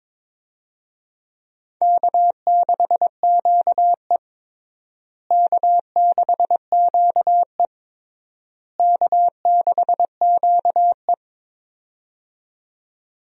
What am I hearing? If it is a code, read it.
Morse code, heard as a keyed tone, about 21 wpm, 700 Hz. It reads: K6QE K6QE K6QE
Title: USA Call Signs - 4 Characters - Repeated 3x - 22wpm